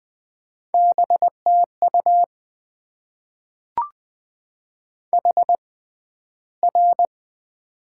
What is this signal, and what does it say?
Morse code, heard as a keyed tone, about 20 wpm, spaced as for 12 wpm, 700 Hz. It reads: BTU HR
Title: Top 90 QSO Elements - 6x Character Spacing - 20wpm